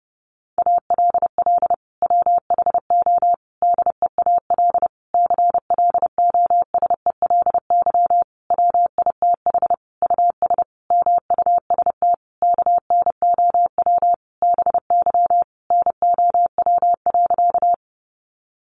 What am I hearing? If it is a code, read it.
Morse code, heard as a keyed tone, about 30 wpm, 700 Hz. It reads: ALL WHO DEAL CLOSELY WITH US MUST KNOW BY NOW.
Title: Sentences from Top 1400 Words 30wpm